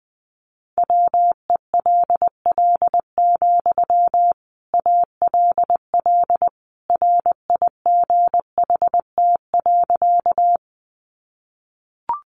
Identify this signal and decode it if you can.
Morse code, heard as a keyed tone, about 20 wpm, 700 Hz. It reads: WELL, ALL RIGHT.
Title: Sentences from Top 200 Words 20wpm